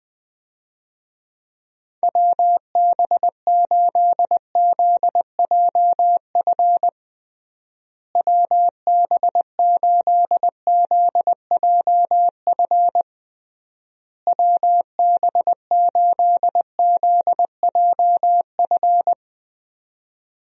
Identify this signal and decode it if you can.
Morse code, heard as a keyed tone, about 20 wpm, 700 Hz. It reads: WB8ZJF WB8ZJF WB8ZJF
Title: USA Call Signs - 6 Characters - Repeated 3x - 20wpm